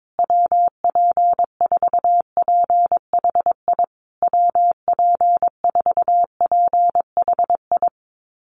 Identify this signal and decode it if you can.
Morse code, heard as a keyed tone, about 22 wpm, 700 Hz. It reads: WP4PHI WP4PHI
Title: USA Call Signs Repeated 3x 22wpm